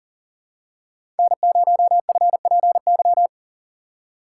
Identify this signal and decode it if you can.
Morse code, heard as a keyed tone, about 39 wpm, 700 Hz. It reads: N0FPY
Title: USA Call Signs - 5 Characters - 6x Character Spacing - 40wpm